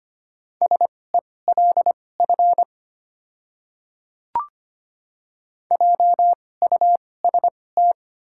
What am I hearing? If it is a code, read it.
Morse code, heard as a keyed tone, about 25 wpm, spaced as for 18 wpm, 690 Hz. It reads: SELF JUST
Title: 2x Character Spacing - Top 200 Words - 25wpm